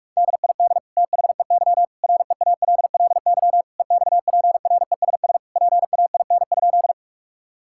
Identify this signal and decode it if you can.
Morse code, heard as a keyed tone, about 46 wpm, 700 Hz. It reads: DID THEY REALLY EXPRESS PAIN?
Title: Sentences from Top 1600 Words 45wpm